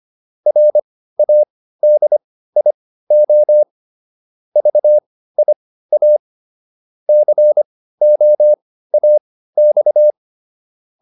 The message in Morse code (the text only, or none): RADIO VIA COAX